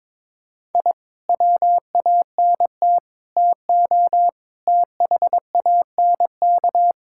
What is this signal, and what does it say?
Morse code, heard as a keyed tone, about 22 wpm, 700 Hz. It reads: I WANT TO THANK
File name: practice / 22wpm / Sentences with 8-Word Phrases - 22wpm